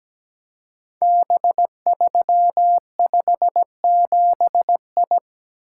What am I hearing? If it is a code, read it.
Morse code, heard as a keyed tone, about 17 wpm, 700 Hz. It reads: B357I
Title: Code Groups - 5 Letter:Numbers - 17wpm